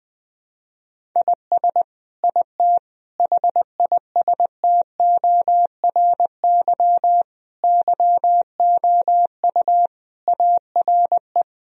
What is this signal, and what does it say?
Morse code, heard as a keyed tone, about 21 wpm, 700 Hz. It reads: IS IT HISTORY YOU ARE
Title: Sentences from Top 1000 Words - Review Part 3 - 20wpm